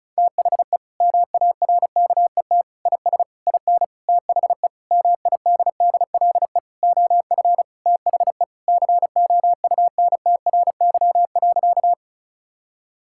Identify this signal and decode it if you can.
Morse code, heard as a keyed tone, about 34 wpm, 700 Hz. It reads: THE MARKET IS IN THE MIDDLE OF THE COUNTRY.
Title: Sentences from Top 1000 Words - Review Part 6 - 35wpm